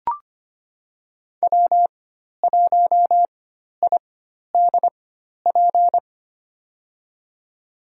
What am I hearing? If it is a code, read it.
Morse code, heard as a keyed tone, about 25 wpm, spaced as for 12 wpm, 700 Hz. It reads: W1IDP
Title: USA Call Signs - 5 Characters - 4x Character Spacing - 25wpm